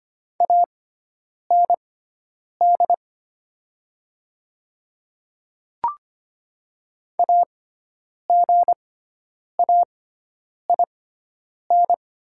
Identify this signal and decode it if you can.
Morse code, heard as a keyed tone, about 25 wpm, spaced as for 9 wpm, 700 Hz. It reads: AND AGAIN
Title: 6x Character Spacing - Top 500 Words - 25wpm